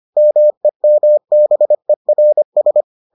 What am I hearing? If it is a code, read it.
Morse code, heard as a keyed tone, about 25 wpm, 600 Hz. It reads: MEMBERS